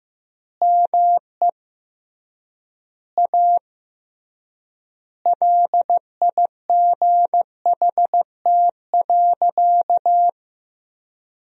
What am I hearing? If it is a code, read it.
Morse code, heard as a keyed tone, about 16 wpm, 700 Hz. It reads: ME A LIGHT.
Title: 3x Word-Spacing - Sentences from Top 200 Words - 15wpm